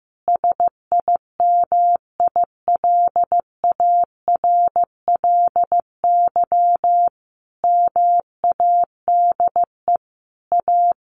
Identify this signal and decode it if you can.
Morse code, heard as a keyed tone, about 15 wpm, 700 Hz. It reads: SIMILARLY MADE A
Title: Sentences from Top 1400 Words 15wpm